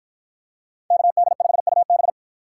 Morse code, heard as a keyed tone, about 53 wpm, 700 Hz. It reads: KD5UB